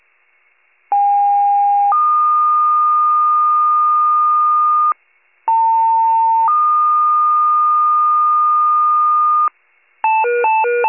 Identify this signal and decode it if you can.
Two-tone page, heard as a one-second tone, then a three-second tone, about 810 then 1220 Hz, about 900 then 1220 Hz. It is heard from 0.9 to 9.5 s.
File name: pager / 73 RESCUE / wcalert